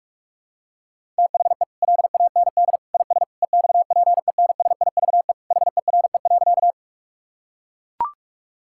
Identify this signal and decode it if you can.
Morse code, heard as a keyed tone, about 41 wpm, 700 Hz. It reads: THE LAND IS EXPENSIVE HERE.
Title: Sentences from Top 1000 Words - Review Part 6 - 45wpm